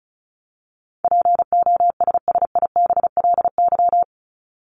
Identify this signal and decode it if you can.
Morse code, heard as a keyed tone, about 34 wpm, 700 Hz. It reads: POSSIBLY
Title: LY Suffix - Top 100 Words with Base Words 35wpm